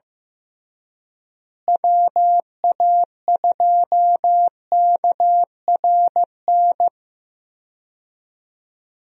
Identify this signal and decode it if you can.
Morse code, heard as a keyed tone, about 15 wpm, 700 Hz. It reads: WA2KRN
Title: USA Call Signs - 6 Characters - Repeated 2x - 15wpm